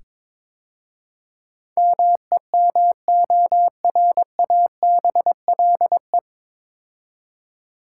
Morse code, heard as a keyed tone, about 22 wpm, 700 Hz. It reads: MEMORABLE
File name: 9 Letter Words - First 3 Letters Given - 22wpm